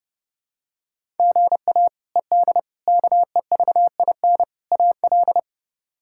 Morse code, heard as a keyed tone, about 30 wpm, 700 Hz. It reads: GA ED KEVIN AL
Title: K1USN SST QSO Practice - 30wpm